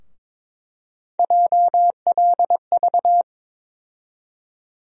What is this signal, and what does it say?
Morse code, heard as a keyed tone, about 22 wpm, 700 Hz. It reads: JLV
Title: USA Call Sign Postfixes - 3 Letters 22wpm